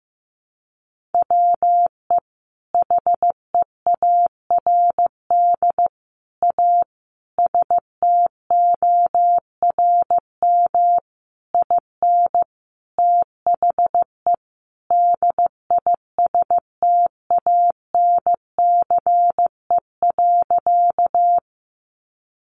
Morse code, heard as a keyed tone, about 15 wpm, 700 Hz. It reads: WE HEARD A STORM IN THE DISTANCE.
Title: Sentences from Top 1500 Words 15wpm